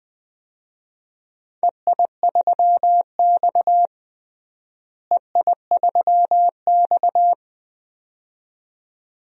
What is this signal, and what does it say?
Morse code, heard as a keyed tone, about 21 wpm, 700 Hz. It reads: EI3X EI3X
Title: DXCC-100 Call Signs - 4 Characters - Repeated 2x - 20wpm